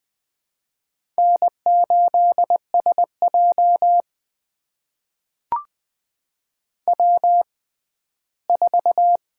Morse code, heard as a keyed tone, about 20 wpm, spaced as for 14 wpm, 700 Hz. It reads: N8SJ W4
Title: USA Call Signs - 4 Characters - 6x Character Spacing - 20wpm